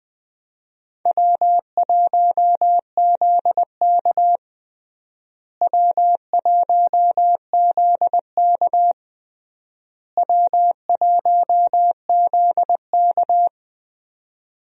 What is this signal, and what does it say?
Morse code, heard as a keyed tone, about 20 wpm, 700 Hz. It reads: W1ZK W1ZK W1ZK
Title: USA Call Signs - 4 Characters - Repeated 3x - 20wpm